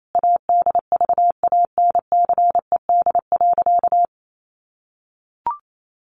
Morse code, heard as a keyed tone, about 28 wpm, 700 Hz. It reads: ADVANCED.
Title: Sentences from Top 1200 Words 28wpm